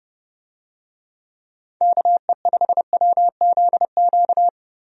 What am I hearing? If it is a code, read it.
Morse code, heard as a keyed tone, about 32 wpm, 700 Hz. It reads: KE5WZQ
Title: USA Call Signs - 6 Characters - Repeated 2x - 30wpm